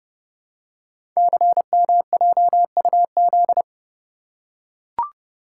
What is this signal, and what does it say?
Morse code, heard as a keyed tone, about 30 wpm, 700 Hz. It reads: CMJUZ
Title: Code Groups - 5 Letters - 30wpm